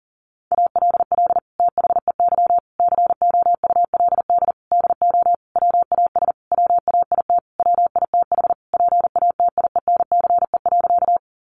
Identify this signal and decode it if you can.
Morse code, heard as a keyed tone, about 40 wpm, 700 Hz. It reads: ALL THEY COULD DO WAS WAIT WITH PATIENCE.
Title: Sentences from Top 1300 Words 40wpm